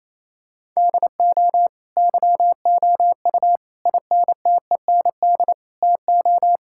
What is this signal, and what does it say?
Morse code, heard as a keyed tone, about 28 wpm, 700 Hz. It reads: DO YOU INTEND TO
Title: Sentences from Top 1600 Words 28wpm